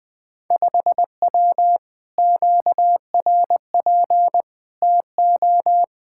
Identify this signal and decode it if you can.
Morse code, heard as a keyed tone, about 20 wpm, 700 Hz. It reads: 5W QRP TO
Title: Speed-Racing Common QSO Phrases Encore - 17wpm